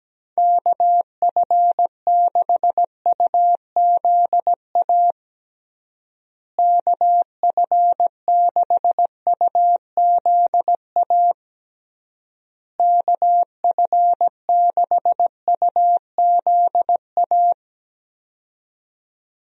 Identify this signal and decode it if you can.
Morse code, heard as a keyed tone, about 17 wpm, 700 Hz. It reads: KF6UZA KF6UZA KF6UZA
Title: USA Call Signs - 6 Characters - Repeated 3x - 17wpm